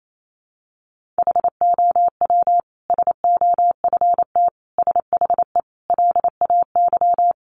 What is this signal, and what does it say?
Morse code, heard as a keyed tone, about 28 wpm, 700 Hz. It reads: HOW SOFT SHE LAY
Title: Sentences from Top 1500 Words 28wpm